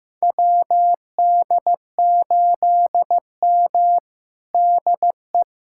Morse code, heard as a keyed tone, about 15 wpm, 700 Hz. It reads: WD8M DE
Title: USA Callsign DE Callsign - 4 Characters - 15wpm